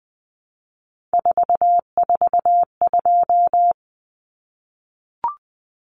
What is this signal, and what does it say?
Morse code, heard as a keyed tone, about 19 wpm, 700 Hz. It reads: 442
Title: Numbers - 3-Digit 20wpm